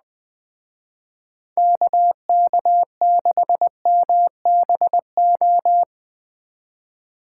Morse code, heard as a keyed tone, about 20 wpm, 700 Hz. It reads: KK6MBO
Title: Speed Racing - USA 2x3 Call Signs 17wpm